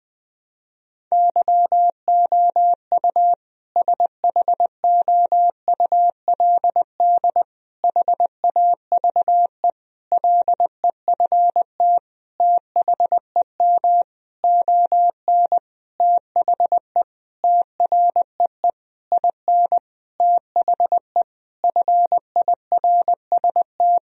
Morse code, heard as a keyed tone, about 20 wpm, 700 Hz. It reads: YOU SHOULD HAVE LEFT THEM ON THE TREE IN THE FIRST
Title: Sentences from Top 1000 Words - Review Part 1 - 20wpm